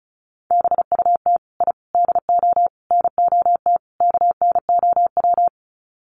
Morse code, heard as a keyed tone, about 34 wpm, 700 Hz. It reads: BUT I DO NOT KNOW
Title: Sets of 5 Words 35wpm